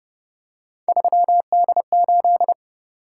Morse code, heard as a keyed tone, about 30 wpm, 700 Hz. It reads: 3D8
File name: Speed-Racing - Three Letter-Number Combinations - 25wpm